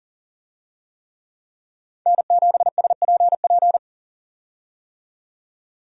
Morse code, heard as a keyed tone, about 40 wpm, 700 Hz. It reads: N7SPP
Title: USA Call Signs - 5 Characters - 40wpm